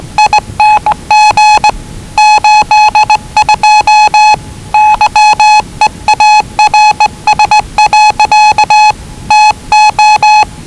Morse code, heard as a keyed tone, about 20 wpm, 840 Hz. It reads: ING 82 YEARS. TO
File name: WellingtonRadioZlwCloseDownOnHfAndMf / 01-Zlw 4277khz 417.5 qrt until 0925z